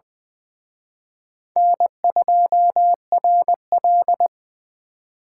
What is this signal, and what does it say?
Morse code, heard as a keyed tone, about 23 wpm, 690 Hz. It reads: N2RL
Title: Speed Racing - USA 1x2 Call Signs 20wpm